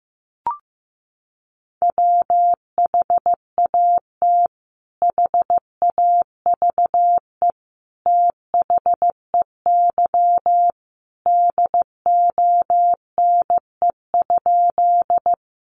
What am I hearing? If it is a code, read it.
Morse code, heard as a keyed tone, about 15 wpm, 700 Hz. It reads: WHAT HAVE THEY DONE?
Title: Sentences from Top 500 Words 15wpm